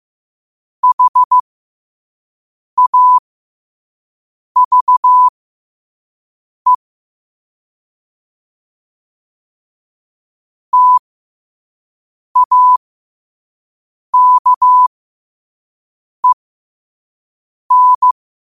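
Morse code, heard as a keyed tone, about 15 wpm, spaced as for 5 wpm, 1000 Hz. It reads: HAVE TAKEN
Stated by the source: Morse code practice at 5 wpm.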